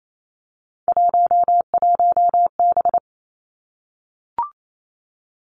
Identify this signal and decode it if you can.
Morse code, heard as a keyed tone, about 28 wpm, 700 Hz. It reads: 11B
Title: Three Challenging Letter-Number Combinations 28wpm